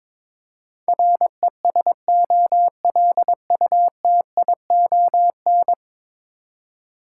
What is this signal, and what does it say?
Morse code, heard as a keyed tone, about 22 wpm, 700 Hz. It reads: RESOLUTION
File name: practice / 22wpm / 10 Letter Words - First 3 Letters Given - 22wpm